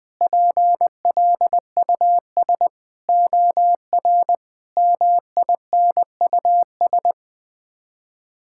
Morse code, heard as a keyed tone, about 20 wpm, 700 Hz. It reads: PLUS OR MINUS
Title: Sets of 8 Words - 20wpm